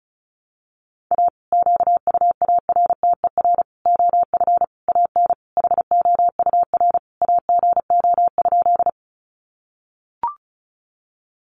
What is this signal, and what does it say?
Morse code, heard as a keyed tone, about 34 wpm, 700 Hz. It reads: A QUARTER OF AN HOUR AGO?
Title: Sentences from Top 1300 Words 35wpm